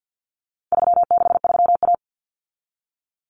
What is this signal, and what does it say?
Morse code, heard as a keyed tone, about 50 wpm, 700 Hz. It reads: 363A
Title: Common Cut Numbers - 4-Digit 50wpm